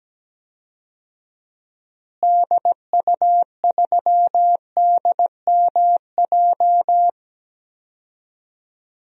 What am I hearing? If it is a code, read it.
Morse code, heard as a keyed tone, about 17 wpm, 700 Hz. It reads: DU3DMJ
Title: DXCC-100 Call Signs - 6 Characters - Repeated 2x - 17wpm